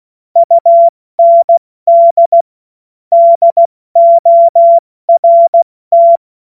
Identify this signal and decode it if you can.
Morse code, heard as a keyed tone, about 16 wpm, 680 Hz. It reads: UND DORT
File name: Morsekurs / Witze / Witz 31. Mai 2022 70 F80